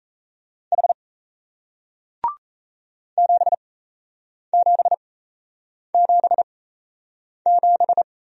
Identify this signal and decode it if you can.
Morse code, heard as a keyed tone, about 38 wpm, spaced as for 8 wpm, 700 Hz. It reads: H 7777